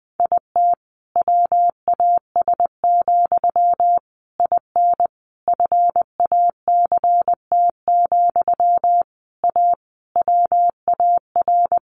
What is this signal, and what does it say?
Morse code, heard as a keyed tone, about 19 wpm, 700 Hz. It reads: IT WAS, IN FACT, A WAR
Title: Sentences from Top 1400 Words 20wpm